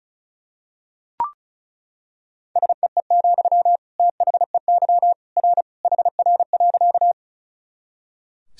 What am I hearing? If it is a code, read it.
Morse code, heard as a keyed tone, about 36 wpm, 700 Hz. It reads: SEE, THEY R HR.
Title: Sentences from Top 200 Words and Abbreviations 35wpm